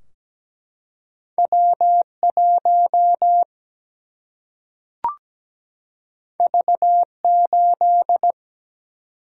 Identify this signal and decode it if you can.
Morse code, heard as a keyed tone, about 17 wpm, 700 Hz. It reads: W 1 V 8
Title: Single-Letter Single-Number 17wpm